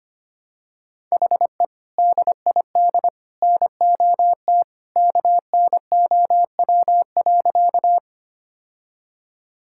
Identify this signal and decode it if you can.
Morse code, heard as a keyed tone, about 25 wpm, 700 Hz. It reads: HE DID NOT KNOW.
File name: practice / 25wpm / Sentences with 2-Word Phrases - 25wpm